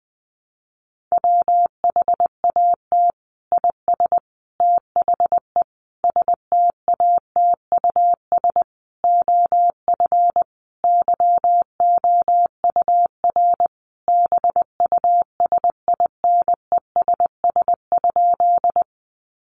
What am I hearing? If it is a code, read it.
Morse code, heard as a keyed tone, about 20 wpm, 700 Hz. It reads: WHAT IS THE STATUS OF YOUR BUSINESS?
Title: Sentences from Top 1400 Words 20wpm